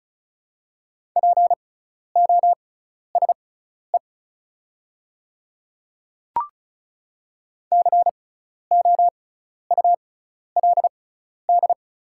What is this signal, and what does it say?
Morse code, heard as a keyed tone, about 36 wpm, spaced as for 12 wpm, 690 Hz. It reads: POSE COULD
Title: 6x Character Spacing - Top 500 Words - 35wpm